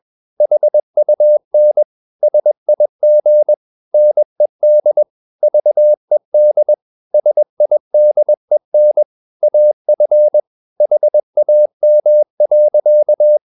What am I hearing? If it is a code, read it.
Morse code, heard as a keyed tone, about 21 wpm, 600 Hz. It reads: HUN SIG NED VED SIDEN AF HAM.
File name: Ferie QRS 21wpm